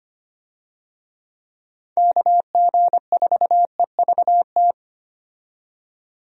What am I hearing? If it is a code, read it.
Morse code, heard as a keyed tone, about 25 wpm, 700 Hz. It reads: KG4EVT
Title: USA Call Signs - 6 Characters - 25wpm